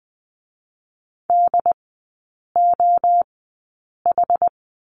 Morse code, heard as a keyed tone, about 20 wpm, spaced as for 8 wpm, 700 Hz. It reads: DOH
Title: 067 Lesson 12e - Letter d - Three Characters 20wpm